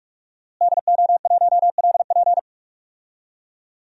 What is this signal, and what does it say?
Morse code, heard as a keyed tone, about 46 wpm, 700 Hz. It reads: DO1LP
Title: DXCC-100 Call Signs - 5 Characters - Repeated 2x - 45wpm